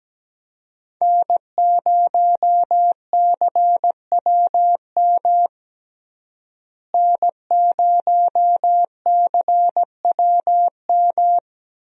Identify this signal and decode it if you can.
Morse code, heard as a keyed tone, about 17 wpm, 700 Hz. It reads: N0CWM N0CWM
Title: USA Call Signs - 5 Characters - Repeated 3x - 17wpm